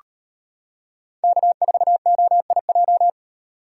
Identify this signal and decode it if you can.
Morse code, heard as a keyed tone, about 32 wpm, 700 Hz. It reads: K4OIJ